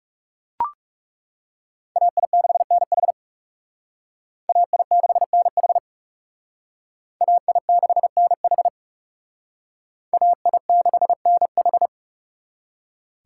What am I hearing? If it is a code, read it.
Morse code, heard as a keyed tone, about 38 wpm, 700 Hz. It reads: AI6NH AI6NH AI6NH AI6NH